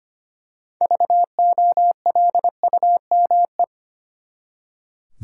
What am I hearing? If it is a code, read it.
Morse code, heard as a keyed tone, about 25 wpm, 700 Hz. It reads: VOLUME
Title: Six Letter Words - 25wpm